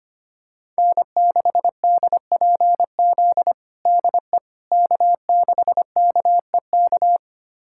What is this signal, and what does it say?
Morse code, heard as a keyed tone, about 25 wpm, 700 Hz. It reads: N6DPZ DE K6KEK
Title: USA Callsign DE Callsign - 5 Characters - 25wpm